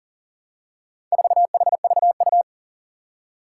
Morse code, heard as a keyed tone, about 38 wpm, 700 Hz. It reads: 4HVU
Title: Four Challenging Letter-Number Combinations 40wpm